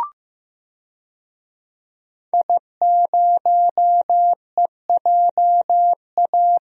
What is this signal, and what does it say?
Morse code, heard as a keyed tone, about 15 wpm, 700 Hz. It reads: I0EJA
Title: DXCC-100 Call Signs - 5 Characters - Repeated 3x - 15wpm